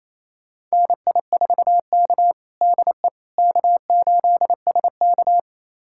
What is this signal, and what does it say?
Morse code, heard as a keyed tone, about 28 wpm, 690 Hz. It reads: NI4K DE K8SK
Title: USA Callsign DE Callsign - 4 Characters - 28wpm